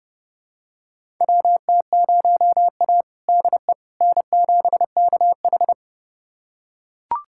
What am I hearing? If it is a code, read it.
Morse code, heard as a keyed tone, about 30 wpm, 700 Hz. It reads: WT0A DE N7KH
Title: USA Callsign DE Callsign - 4 Characters - 30wpm